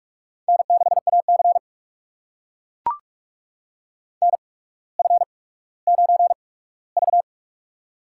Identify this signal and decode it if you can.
Morse code, heard as a keyed tone, about 44 wpm, spaced as for 19 wpm, 700 Hz. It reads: N6AC NF9V
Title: USA Call Signs - 4 Characters - 8x Character Spacing - 45wpm